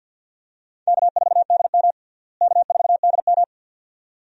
Morse code, heard as a keyed tone, about 52 wpm, 700 Hz. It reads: K4DM K4DM
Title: USA Call Signs - 4 Characters - Repeated 2x - 50wpm